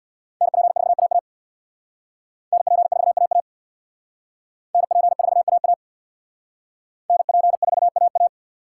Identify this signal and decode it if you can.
Morse code, heard as a keyed tone, about 59 wpm, 700 Hz. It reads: NP4AA NP4AA NP4AA NP4AA
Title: Speed Racing - USA 2x2 Call Signs 50wpm